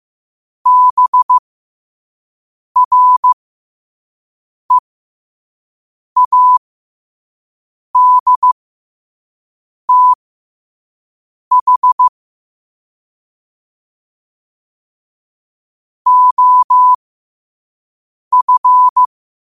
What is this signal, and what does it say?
Morse code, heard as a keyed tone, about 15 wpm, spaced as for 5 wpm, 1000 Hz. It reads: BREADTH OF
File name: qotdfiles / 5wpm